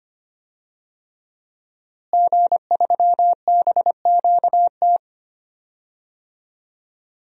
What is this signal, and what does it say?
Morse code, heard as a keyed tone, about 25 wpm, 700 Hz. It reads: G3BQT
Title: DXCC-100 Call Signs - 5 Characters - Repeated 3x - 25wpm